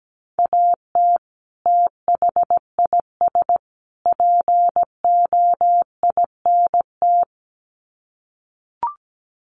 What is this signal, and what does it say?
Morse code, heard as a keyed tone, about 17 wpm, 700 Hz. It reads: AT THIS POINT
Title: Sets of 3 Words 17wpm